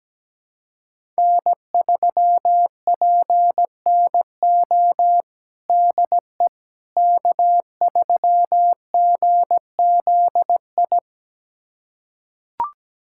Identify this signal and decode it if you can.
Morse code, heard as a keyed tone, about 17 wpm, 700 Hz. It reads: N3PNO DE K3GZI
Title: USA Callsign DE Callsign - 5 Characters - 17wpm